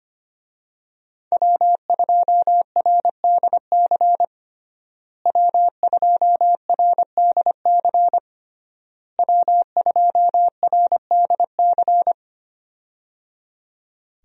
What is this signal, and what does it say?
Morse code, heard as a keyed tone, about 25 wpm, 700 Hz. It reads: W2RDC W2RDC W2RDC
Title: USA Call Signs - 5 Characters - Repeated 3x - 25wpm